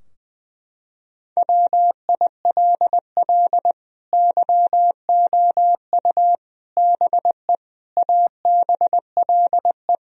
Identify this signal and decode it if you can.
Morse code, heard as a keyed tone, about 21 wpm, 700 Hz. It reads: WILL YOU BE ABLE
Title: Sentences from Top 1000 Words - Review Part 2 - 20wpm